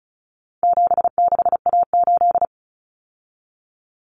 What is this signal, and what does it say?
Morse code, heard as a keyed tone, about 34 wpm, 700 Hz. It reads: 76A8
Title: Common Cut Numbers - 4-Digit 35wpm